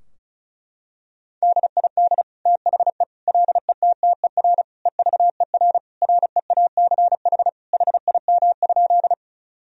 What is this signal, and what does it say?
Morse code, heard as a keyed tone, about 36 wpm, 700 Hz. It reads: DID THE LETTER EVER REACH HIM?
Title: Sentences from Top 1000 Words - Review Part 2 - 35wpm